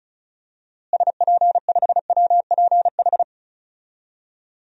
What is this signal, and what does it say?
Morse code, heard as a keyed tone, about 36 wpm, 700 Hz. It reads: SP5WPH
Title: DXCC-100 Call Signs - 6 Characters - Repeated 3x - 35wpm